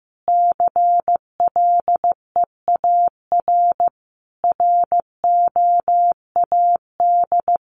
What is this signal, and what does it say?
Morse code, heard as a keyed tone, about 15 wpm, 700 Hz. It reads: CLEAR ROAD
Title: Sentences from Top 800 Words 15wpm